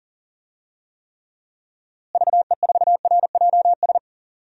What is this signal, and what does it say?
Morse code, heard as a keyed tone, about 37 wpm, 700 Hz. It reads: VE4RJS
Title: DXCC-100 Call Signs - 6 Characters - Repeated 3x - 40wpm